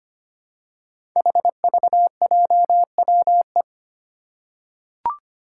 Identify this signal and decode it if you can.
Morse code, heard as a keyed tone, about 25 wpm, 700 Hz. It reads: HVJWE
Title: Five Challenging Letter-Number Combinations 25wpm